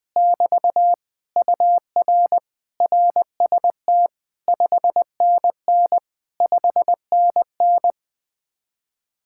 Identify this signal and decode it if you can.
Morse code, heard as a keyed tone, about 20 wpm, 700 Hz. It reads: <BT> UR RST 5NN 5NN